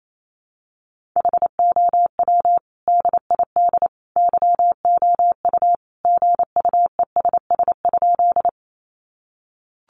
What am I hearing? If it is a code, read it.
Morse code, heard as a keyed tone, about 28 wpm, 700 Hz. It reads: HOW DID YOU GUESS?
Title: Sentences from Top 1500 Words 28wpm